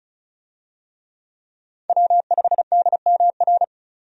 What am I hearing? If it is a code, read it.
Morse code, heard as a keyed tone, about 34 wpm, 700 Hz. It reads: W5DMR
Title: USA Call Signs - 5 Characters - 35wpm